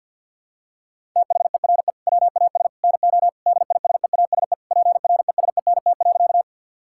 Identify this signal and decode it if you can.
Morse code, heard as a keyed tone, about 50 wpm, 700 Hz. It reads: THERE WAS NO DISEASE PRESENT.
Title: Sentences from Top 1000 Words - Review Part 6 - 50wpm